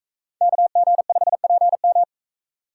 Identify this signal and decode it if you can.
Morse code, heard as a keyed tone, about 41 wpm, 700 Hz. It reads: KG5PM